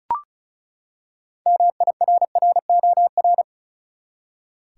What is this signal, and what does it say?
Morse code, heard as a keyed tone, about 36 wpm, 700 Hz. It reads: MIRROR
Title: Six Letter Words - 35wpm